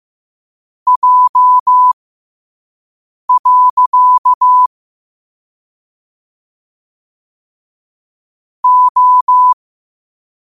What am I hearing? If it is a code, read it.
Morse code, heard as a keyed tone, about 15 wpm, spaced as for 5 wpm, 1000 Hz. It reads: J. O
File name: 5wpm